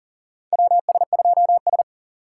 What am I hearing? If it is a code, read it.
Morse code, heard as a keyed tone, about 40 wpm, 700 Hz. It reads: WS2S
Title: USA Call Signs - 4 Characters - 4x Character Spacing - 40wpm